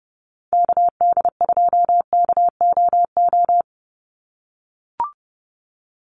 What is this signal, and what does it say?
Morse code, heard as a keyed tone, about 30 wpm, 700 Hz. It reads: KD2KOO
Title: USA Call Signs 30wpm